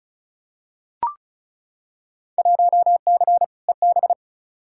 Morse code, heard as a keyed tone, about 36 wpm, 700 Hz. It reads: 1C EB
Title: Field Day Exchanges Encore - 35wpm